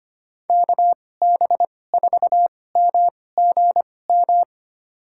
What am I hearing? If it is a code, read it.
Morse code, heard as a keyed tone, about 25 wpm, spaced as for 18 wpm, 700 Hz. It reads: KB4MGM
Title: USA Call Signs - 6 Characters - 2x Character Spacing - 25wpm